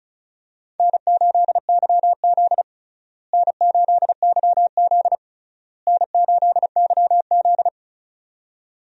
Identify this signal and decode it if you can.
Morse code, heard as a keyed tone, about 36 wpm, 700 Hz. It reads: N8YZ N8YZ N8YZ
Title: USA Call Signs - 4 Characters - Repeated 3x - 35wpm